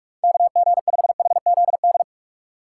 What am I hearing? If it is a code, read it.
Morse code, heard as a keyed tone, about 46 wpm, 700 Hz. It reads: KG5HZD
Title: USA Call Signs - 6 Characters - 2x Character Spacing - 45wpm